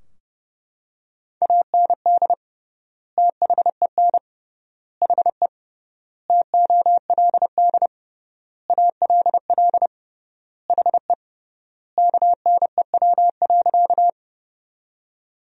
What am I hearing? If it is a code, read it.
Morse code, heard as a keyed tone, about 32 wpm, 700 Hz. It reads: AND THEN HE TOLD ALL HE KNEW.
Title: 3x Word-Spacing - Sentences from Top 400 Words - 30wpm